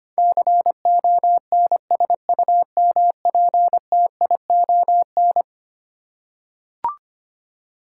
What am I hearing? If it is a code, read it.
Morse code, heard as a keyed tone, about 25 wpm, 700 Hz. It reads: CONSUMPTION
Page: morse-code-ninja-practice-25wpm